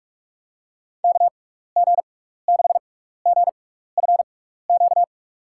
Morse code, heard as a keyed tone, about 46 wpm, spaced as for 16 wpm, 700 Hz. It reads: KG6GFQ
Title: USA Call Signs - 6 Characters - 6x Character Spacing - 45wpm